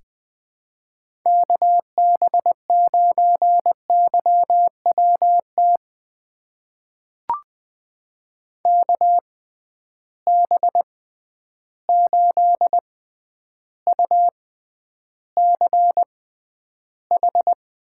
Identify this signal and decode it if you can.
Morse code, heard as a keyed tone, about 20 wpm, spaced as for 10 wpm, 700 Hz. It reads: KB9YWT KB8UCH
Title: USA Call Signs - 6 Characters - 6x Character Spacing - 20wpm